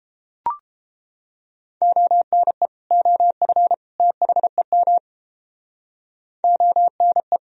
Morse code, heard as a keyed tone, about 30 wpm, 700 Hz. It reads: ONE OF THEM ONE